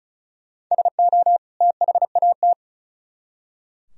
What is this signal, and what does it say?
Morse code, heard as a keyed tone, about 47 wpm, spaced as for 41 wpm, 700 Hz. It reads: SO THAT
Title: Speed-Racing - Sets of 2 Words 35wpm